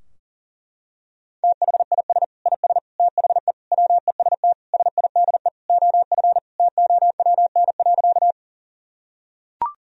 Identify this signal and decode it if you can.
Morse code, heard as a keyed tone, about 40 wpm, 700 Hz. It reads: THIS IS THE WEST SIDE OF TOWN.
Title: Sentences from Top 1000 Words - Review Part 2 - 40wpm